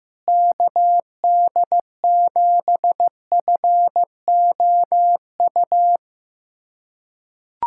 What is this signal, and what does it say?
Morse code, heard as a keyed tone, about 15 wpm, 700 Hz. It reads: KD7FOU
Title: USA Callsign DE Callsign 15wpm